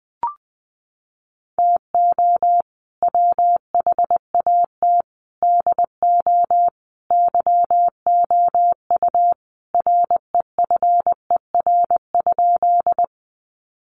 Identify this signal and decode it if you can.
Morse code, heard as a keyed tone, about 20 wpm, 700 Hz. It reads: TO WHAT DO YOU REFER?
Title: Sentences from Top 1000 Words 20wpm